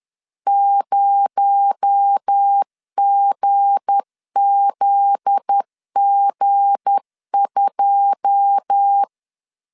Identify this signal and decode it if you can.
Morse code, heard as a keyed tone, about 10 wpm, 800 Hz. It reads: 0GZG2
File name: LEZIONI / 9 2867 10wpm 2733